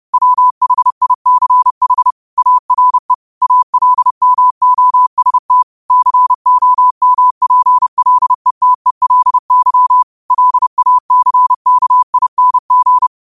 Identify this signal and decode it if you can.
Morse code, heard as a keyed tone, about 30 wpm, 1000 Hz. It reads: WHICH ARE ALMOST COMPLETELY LACKING
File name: qotdfiles / 30wpm